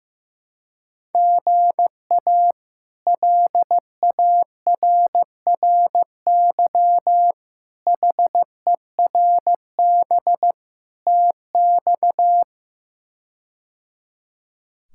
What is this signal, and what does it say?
Morse code, heard as a keyed tone, about 15 wpm, 700 Hz. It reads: GA LARRY HERB TX
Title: K1USN SST QSO Practice - 15wpm